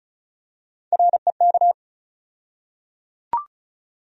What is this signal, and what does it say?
Morse code, heard as a keyed tone, about 34 wpm, 700 Hz. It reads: REK
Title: USA Call Sign Postfixes - 3 Letters 35wpm